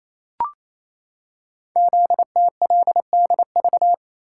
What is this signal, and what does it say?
Morse code, heard as a keyed tone, about 28 wpm, 700 Hz. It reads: ZTLDV
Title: Code Groups - 5 Letters - 28wpm